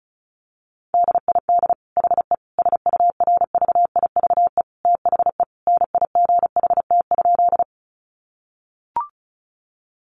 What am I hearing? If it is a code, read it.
Morse code, heard as a keyed tone, about 34 wpm, 700 Hz. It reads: DID HE SURVIVE THE NIGHT?
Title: Sentences from Top 1300 Words 35wpm